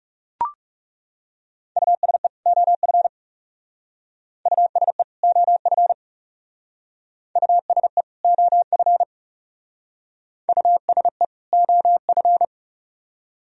Morse code, heard as a keyed tone, about 38 wpm, 700 Hz. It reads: USE OF USE OF USE OF USE OF
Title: Speed-Racing - Sets of 2 Words 30wpm